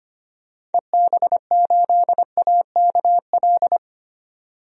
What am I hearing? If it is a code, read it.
Morse code, heard as a keyed tone, about 25 wpm, 700 Hz. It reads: EB8AKL
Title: DXCC-100 Call Signs - 6 Characters - 25wpm